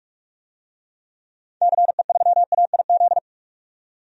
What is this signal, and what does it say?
Morse code, heard as a keyed tone, about 41 wpm, 700 Hz. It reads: CE3AIZ
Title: DXCC-100 Call Signs - 6 Characters - 45wpm